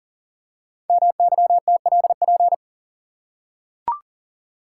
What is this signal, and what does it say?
Morse code, heard as a keyed tone, about 39 wpm, 690 Hz. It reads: MYTLP
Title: Code Groups - 5 Letters - 40wpm